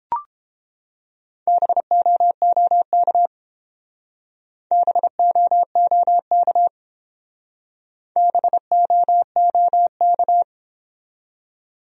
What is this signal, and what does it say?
Morse code, heard as a keyed tone, about 29 wpm, 700 Hz. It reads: BOOK BOOK BOOK